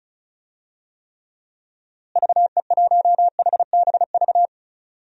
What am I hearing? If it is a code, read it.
Morse code, heard as a keyed tone, about 36 wpm, 700 Hz. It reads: VE1HBV
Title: DXCC-100 Call Signs - 6 Characters - Repeated 3x - 35wpm